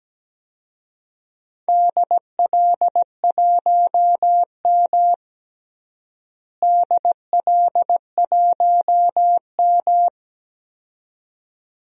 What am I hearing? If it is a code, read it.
Morse code, heard as a keyed tone, about 17 wpm, 700 Hz. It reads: DL1M DL1M
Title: DXCC-100 Call Signs - 4 Characters - Repeated 2x - 17wpm